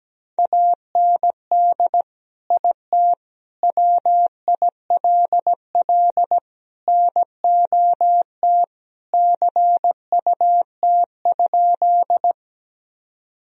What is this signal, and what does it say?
Morse code, heard as a keyed tone, about 17 wpm, 700 Hz. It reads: AND IT WILL NOT CUT?
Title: Sentences from Top 1000 Words - Review Part 1 - 17wpm